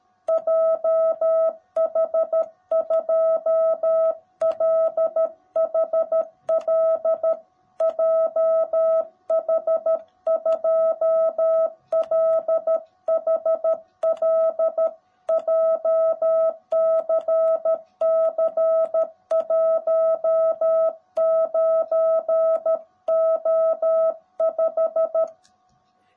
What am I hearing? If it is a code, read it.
Morse code, heard as a keyed tone, about 13 wpm, 650 Hz. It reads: JH2LHL JH2LHL JCC19O5
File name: jh8cbh / 2021cw / cw18b